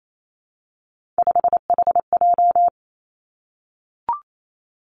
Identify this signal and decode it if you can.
Morse code, heard as a keyed tone, about 28 wpm, 700 Hz. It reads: 5HJ
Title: Three Challenging Letter-Number Combinations 28wpm